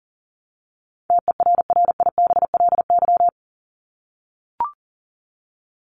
Morse code, heard as a keyed tone, about 40 wpm, 700 Hz. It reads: TERRIBLY
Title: Top 1751-2000 Words 40wpm